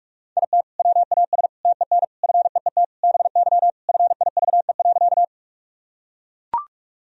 Morse code, heard as a keyed tone, about 41 wpm, 700 Hz. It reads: IT WAS TEN FEET BY FIVE.
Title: Sentences from Top 1000 Words - Review Part 2 - 45wpm